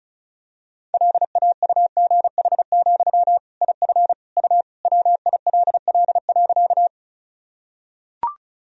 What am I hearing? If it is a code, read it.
Morse code, heard as a keyed tone, about 36 wpm, 700 Hz. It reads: LAUGH, IF U WILL.
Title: Sentences from Top 500 Words and Abbreviations 35wpm